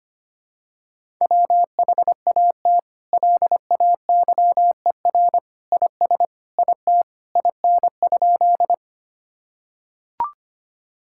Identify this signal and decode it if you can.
Morse code, heard as a keyed tone, about 25 wpm, 700 Hz. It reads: WHAT LAYER IS IT IN?
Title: Sentences from Top 1600 Words 25wpm